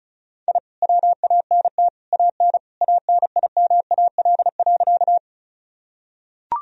Morse code, heard as a keyed tone, about 36 wpm, 700 Hz. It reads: I WANT AN ANIMAL.
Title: Sentences from Top 1000 Words - Review Part 1 - 35wpm